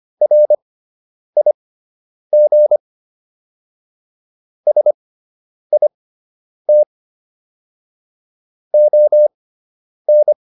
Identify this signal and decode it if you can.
Morse code, heard as a keyed tone, about 25 wpm, spaced as for 9 wpm, 600 Hz. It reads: RIG SIT ON